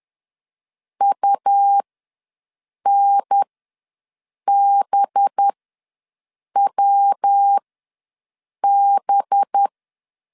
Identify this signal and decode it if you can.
Morse code, heard as a keyed tone, about 10 wpm, spaced as for 6 wpm, 800 Hz. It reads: UNBWB
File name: LEZIONI / 5 LBVC / 5 LB ws 10wpm 3616